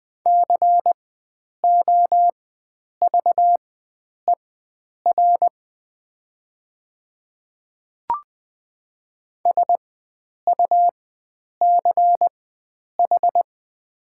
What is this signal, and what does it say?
Morse code, heard as a keyed tone, about 20 wpm, spaced as for 9 wpm, 700 Hz. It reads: COVER SUCH
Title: 4x Character Spacing - Top 300 Words - 20wpm